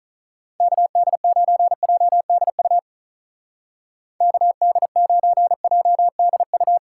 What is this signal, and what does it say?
Morse code, heard as a keyed tone, about 45 wpm, 700 Hz. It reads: KD9JDU KD9JDU